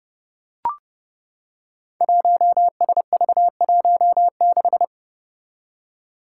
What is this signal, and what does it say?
Morse code, heard as a keyed tone, about 30 wpm, 700 Hz. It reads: 1SV16
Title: Five Challenging Letter-Number Combinations 30wpm